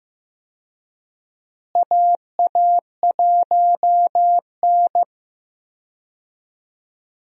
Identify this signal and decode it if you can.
Morse code, heard as a keyed tone, about 15 wpm, 700 Hz. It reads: AA1N
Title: USA Call Signs - 4 Characters - Repeated 3x - 15wpm